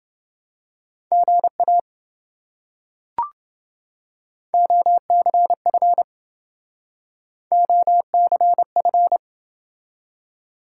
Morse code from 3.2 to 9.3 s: OCF OCF

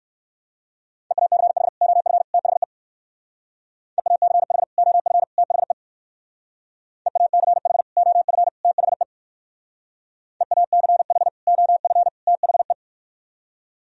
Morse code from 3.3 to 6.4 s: EACH OF THE

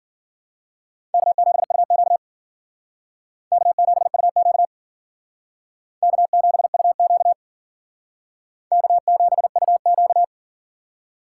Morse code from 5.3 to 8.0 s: K7UQ

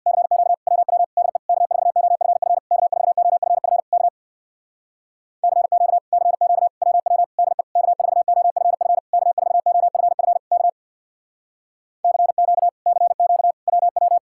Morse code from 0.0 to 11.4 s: CQ WW DE K4OFF K4OFF K CQ CQ WW DE K4OFF K4OFF K

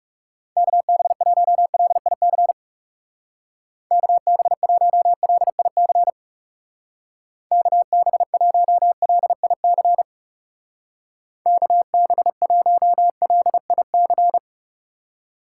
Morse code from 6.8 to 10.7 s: KB1LIC

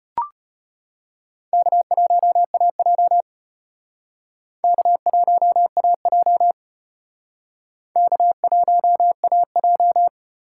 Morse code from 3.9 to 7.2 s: K1AJ

